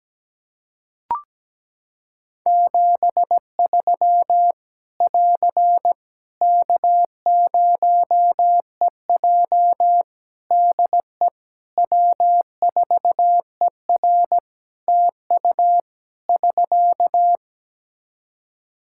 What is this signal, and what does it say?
Morse code, heard as a keyed tone, about 17 wpm, 700 Hz. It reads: 73 <AR> K0EJ DE W4ER TU <SK>
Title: Practice QSO - Exchange 4 - Ending - 17wpm